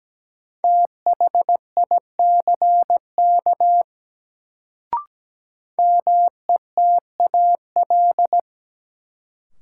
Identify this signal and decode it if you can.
Morse code, heard as a keyed tone, about 17 wpm, 700 Hz. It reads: THICK METAL
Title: Sentences from Top 1600 Words - New Word Drill Alt - 17wpm